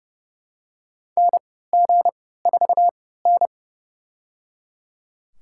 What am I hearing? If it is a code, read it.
Morse code, heard as a keyed tone, about 31 wpm, spaced as for 17 wpm, 700 Hz. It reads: NG4N
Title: USA Call Signs - 3x Character Spacing - 30wpm